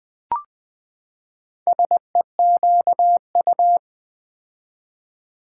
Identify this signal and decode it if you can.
Morse code, heard as a keyed tone, about 20 wpm, 700 Hz. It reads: SEQU
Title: Build A Word - 4 Letters At A Time - 8-Letter Words - 20wpm